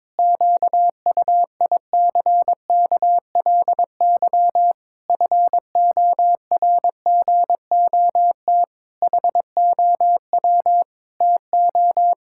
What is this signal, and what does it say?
Morse code, heard as a keyed tone, about 22 wpm, 700 Hz. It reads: QUICKLY FORGOT HOW TO
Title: Pangram Challenge v2 - 22wpm